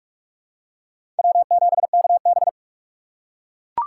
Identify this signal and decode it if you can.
Morse code, heard as a keyed tone, about 43 wpm, 700 Hz. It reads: W7KB
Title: USA Call Signs - 2x Character Spacing - 45wpm